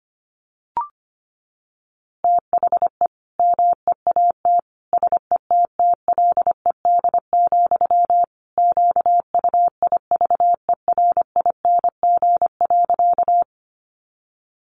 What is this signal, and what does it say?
Morse code, heard as a keyed tone, about 25 wpm, 700 Hz. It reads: THE MEAT SETTLED, QUIVERING.
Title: A Sound Of Thunder 25wpm